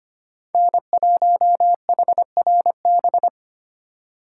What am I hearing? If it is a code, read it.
Morse code, heard as a keyed tone, about 30 wpm, 700 Hz. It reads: N1HRB
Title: Speed Racing - USA 1x3 Call Signs 25wpm